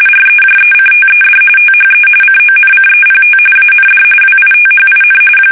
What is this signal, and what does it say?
RTTY, about 45.45 baud. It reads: REJECTED THE CONCEPT AND WITHIN
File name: fsk850 / battleships